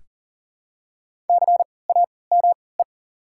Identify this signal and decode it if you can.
Morse code, heard as a keyed tone, about 37 wpm, spaced as for 22 wpm, 700 Hz. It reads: CAME
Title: 3x Character Spacing - Top 200 Words - 40wpm